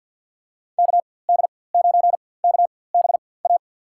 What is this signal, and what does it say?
Morse code, heard as a keyed tone, about 52 wpm, spaced as for 23 wpm, 700 Hz. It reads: KD9KBA
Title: USA Call Signs - 6 Characters - 4x Character Spacing - 50wpm